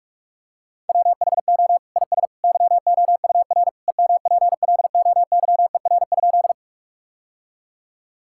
Morse code, heard as a keyed tone, about 42 wpm, 700 Hz. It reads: WHO IS YOUR EMPLOYER?
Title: Sentences from Top 1600 Words 45wpm